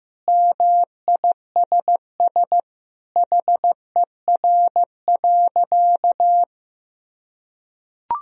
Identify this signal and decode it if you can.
Morse code, heard as a keyed tone, about 16 wpm, 700 Hz. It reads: MISS HER.
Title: Sentences from Top 1000 Words - Review Part 3 - 15wpm